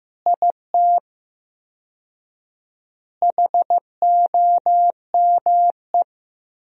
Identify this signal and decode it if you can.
Morse code, heard as a keyed tone, about 15 wpm, 700 Hz. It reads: IT HOME
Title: 4x Word-Spacing - Sentences from Top 500 Words - 15wpm